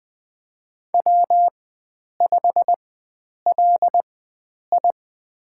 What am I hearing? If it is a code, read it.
Morse code, heard as a keyed tone, about 20 wpm, spaced as for 9 wpm, 700 Hz. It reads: W5LI
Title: USA Call Signs - 4 Characters - 4x Character Spacing - 20wpm